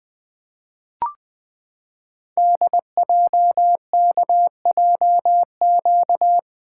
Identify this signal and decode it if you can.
Morse code, heard as a keyed tone, about 20 wpm, 700 Hz. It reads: DJKJQ
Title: Code Groups - 5 Letters - 20wpm